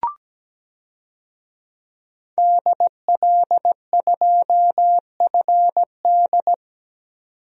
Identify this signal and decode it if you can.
Morse code, heard as a keyed tone, about 17 wpm, 700 Hz. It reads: DL2FD
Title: DXCC-100 Call Signs - 5 Characters - 17wpm